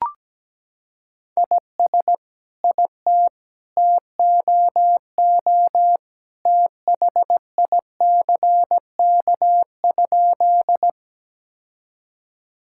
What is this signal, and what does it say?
Morse code, heard as a keyed tone, about 17 wpm, 700 Hz. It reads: IS IT TOO THICK?
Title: Sentences from Top 1600 Words 17wpm